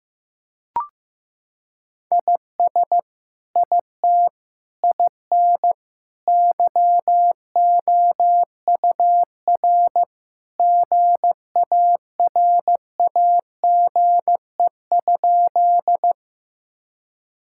Morse code, heard as a keyed tone, about 15 wpm, 700 Hz. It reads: IS IT IN YOUR GARAGE?
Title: Sentences from Top 1600 Words 15wpm